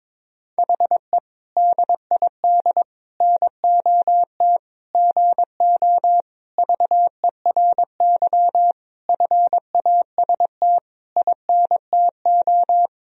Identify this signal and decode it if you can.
Morse code, heard as a keyed tone, about 22 wpm, 700 Hz. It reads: HE DID NOT GO VERY FAST INTO
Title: Sentences from Top 1000 Words - Review Part 2 - 22wpm